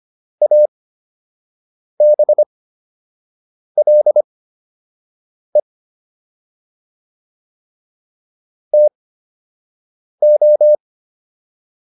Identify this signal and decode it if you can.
Morse code, heard as a keyed tone, about 25 wpm, spaced as for 6 wpm, 600 Hz. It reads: ABLE TO